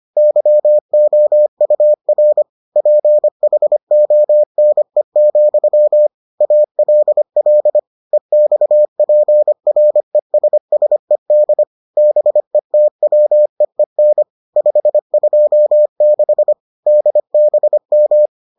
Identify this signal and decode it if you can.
Morse code, heard as a keyed tone, about 25 wpm, 600 Hz. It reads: YOUR PHONE, ALL EXPRESSED BETWEEN 526 DBM